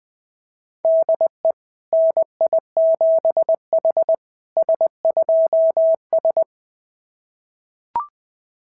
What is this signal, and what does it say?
Morse code, heard as a keyed tone, about 20 wpm, 660 Hz. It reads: DE NI7H S2S
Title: SOTA QSO Practice 20wpm